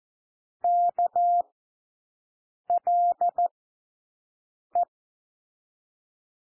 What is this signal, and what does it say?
Morse code, heard as a keyed tone, about 14 wpm, 700 Hz. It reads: K L E
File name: cwmp3 / W1 SOUND WORDS 14 wpm 0